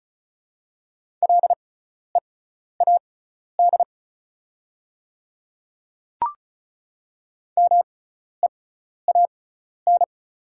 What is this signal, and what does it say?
Morse code, heard as a keyed tone, about 36 wpm, spaced as for 12 wpm, 690 Hz. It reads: LEAD MEAN
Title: 6x Character Spacing - Top 500 Words - 35wpm